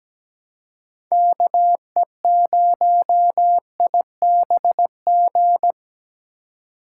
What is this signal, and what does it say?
Morse code, heard as a keyed tone, about 20 wpm, 700 Hz. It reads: KE0IBG
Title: Speed Racing - USA 2x3 Call Signs 17wpm